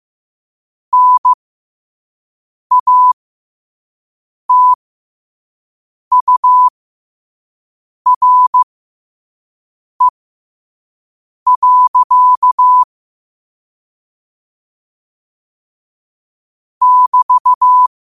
Morse code, heard as a keyed tone, about 15 wpm, spaced as for 5 wpm, 1000 Hz. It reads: NATURE. <BT>